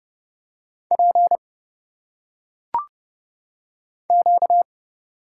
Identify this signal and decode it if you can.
Morse code, heard as a keyed tone, about 30 wpm, 700 Hz. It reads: P Q
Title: Single Letters - Phonetic Pronunciation 30wpm